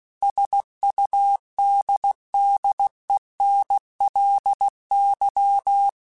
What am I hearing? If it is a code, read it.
Morse code, heard as a keyed tone, about 16 wpm, 800 Hz. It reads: SUDDENLY
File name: Technical Subjects / Technical Subjects-mp3s / AND Gate @15